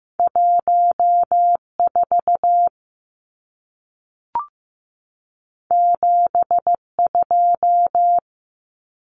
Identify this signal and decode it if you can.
Morse code, heard as a keyed tone, about 15 wpm, 700 Hz. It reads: 1 4 7 2
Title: Numbers - 2-Digit 15wpm